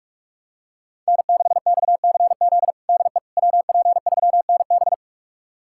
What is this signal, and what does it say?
Morse code, heard as a keyed tone, about 46 wpm, 700 Hz. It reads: N6XCZ DE WP3NB
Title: USA Callsign DE Callsign - 5 Characters - 45wpm